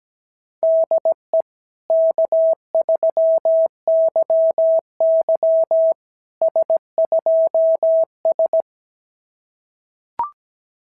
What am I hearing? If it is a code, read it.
Morse code, heard as a keyed tone, about 17 wpm, 690 Hz. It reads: DE K3YY S2S
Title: SOTA QSO Practice 17wpm